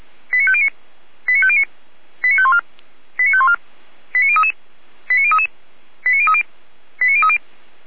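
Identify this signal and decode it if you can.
Five-tone selective call (ZVEI), heard as quick runs of five tones, tones of about 70 ms. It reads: 89509 89509 89514 89514 89033 89033 89030 89030
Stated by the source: ZVEI five-tone selective call 89099, received by SDR, from the server of feuerwehr-orlamuende (fire brigade Orlamünde).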